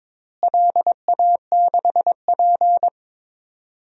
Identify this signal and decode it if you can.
Morse code, heard as a keyed tone, about 21 wpm, 700 Hz. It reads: LA6P
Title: DXCC-100 Call Signs - 4 Characters - Repeated 2x - 22wpm